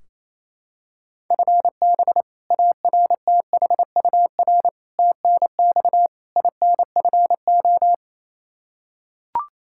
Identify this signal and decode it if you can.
Morse code, heard as a keyed tone, about 28 wpm, 700 Hz. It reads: FB ARTHUR TNX INFO
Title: Practice QSO - Exchange 3 - More Info - 28wpm